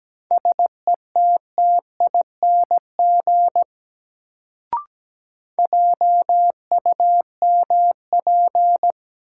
Morse code, heard as a keyed tone, about 17 wpm, 700 Hz. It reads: SETTING JUMP
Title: Sentences from Top 1300 Words - New Word Drill Alt - 17wpm